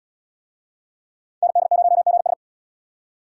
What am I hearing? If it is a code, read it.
Morse code, heard as a keyed tone, about 78 wpm, 700 Hz. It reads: ND0GS